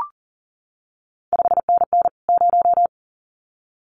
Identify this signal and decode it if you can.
Morse code, heard as a keyed tone, about 38 wpm, 700 Hz. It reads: 5NN 0
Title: CQ WW DX Contest Exchange Encore - 40wpm